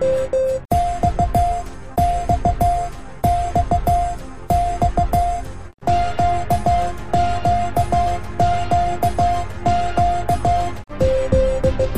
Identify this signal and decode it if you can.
Morse code, heard as a keyed tone, about 15 wpm, 700 Hz, with noise. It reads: X X X X QQQQ
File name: cwa / MorseSongSequence4-K-J-X-Q-Z